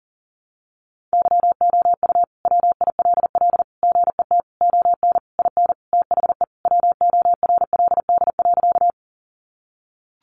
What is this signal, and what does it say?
Morse code, heard as a keyed tone, about 40 wpm, 700 Hz. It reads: YOU WILL GET ON IN THE WORLD.
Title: Sentences from Top 200 Words 40wpm